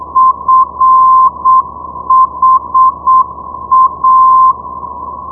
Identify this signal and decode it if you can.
Morse code, heard as a keyed tone, about 7 wpm, 1040 Hz. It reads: FHA
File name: ndb friedrichshafen fha 473.0